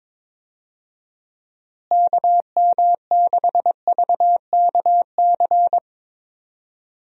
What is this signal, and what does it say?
Morse code, heard as a keyed tone, about 22 wpm, 700 Hz. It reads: KM6VKC
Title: USA Call Signs - 6 Characters - Repeated 2x - 22wpm